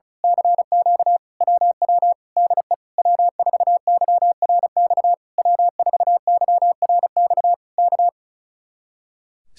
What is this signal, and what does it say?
Morse code, heard as a keyed tone, about 33 wpm, 700 Hz. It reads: CQ WW DE W4YRX W4YRX K
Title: Speed-Racing Common QSO Phrases Encore - 35wpm